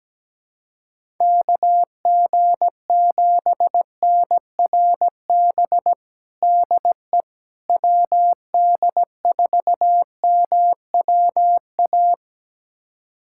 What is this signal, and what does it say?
Morse code, heard as a keyed tone, about 17 wpm, 700 Hz. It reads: KG7NRB DE WD4MWA
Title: USA Callsign DE Callsign 17wpm